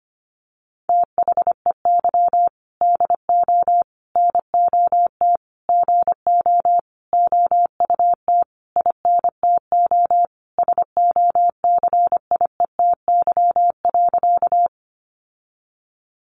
Morse code, heard as a keyed tone, about 25 wpm, 700 Hz. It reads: THEY DO NOT GO OUT INTO SOCIETY.
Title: Sentences from Top 700 Words 25wpm